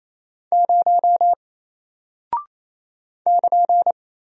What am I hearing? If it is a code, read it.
Morse code, heard as a keyed tone, about 28 wpm, 700 Hz. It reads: 0 <KN>
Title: Single Character (Letter:Number:Punctuation:Prosign) Alt 28wpm